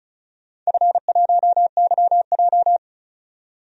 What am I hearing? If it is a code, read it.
Morse code, heard as a keyed tone, about 34 wpm, 700 Hz. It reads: F1YJ
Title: DXCC-100 Call Signs - 4 Characters - 35wpm